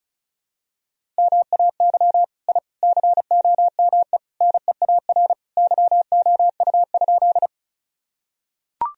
Morse code, heard as a keyed tone, about 36 wpm, 700 Hz. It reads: MAY I COME NEAR YOU?
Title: Sentences from Top 1000 Words - Review Part 1 - 35wpm